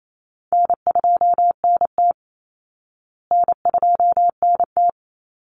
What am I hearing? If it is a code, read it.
Morse code, heard as a keyed tone, about 28 wpm, 700 Hz. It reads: N2NT N2NT
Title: Easy Call Signs Repeated 2x - 28wpm